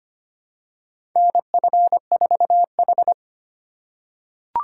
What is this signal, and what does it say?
Morse code, heard as a keyed tone, about 25 wpm, 700 Hz. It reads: NF4H
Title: USA Call Signs - 4 Characters - 3x Character Spacing - 25wpm